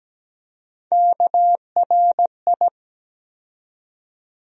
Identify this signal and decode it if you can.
Morse code, heard as a keyed tone, about 17 wpm, 700 Hz. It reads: KRI
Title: USA Call Sign Postfixes - 3 Letters 17wpm